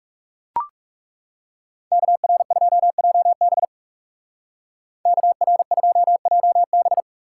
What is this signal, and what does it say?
Morse code, heard as a keyed tone, about 36 wpm, 700 Hz. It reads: KR2JB KR2JB
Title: Speed Racing - USA 2x2 Call Signs 30wpm